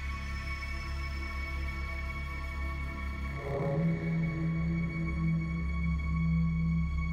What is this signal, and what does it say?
Dial-up modem handshake.